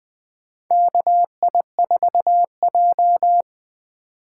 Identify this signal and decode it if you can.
Morse code, heard as a keyed tone, about 21 wpm, 700 Hz. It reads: KI4J
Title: USA Call Signs - 4 Characters - 4x Character Spacing - 20wpm